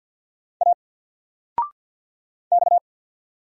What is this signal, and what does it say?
Morse code, heard as a keyed tone, about 50 wpm, 700 Hz. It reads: A X
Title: Single Letter-Number Alt 50wpm